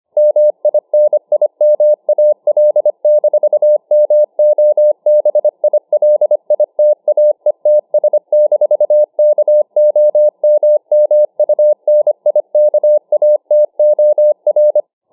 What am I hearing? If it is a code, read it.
Morse code, heard as a keyed tone, about 25 wpm, 600 Hz. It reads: MINIMAL-MOBILITAETS-KOMMUNIKATOR